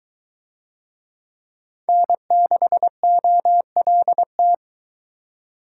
Morse code, heard as a keyed tone, about 22 wpm, 700 Hz. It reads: N6OLT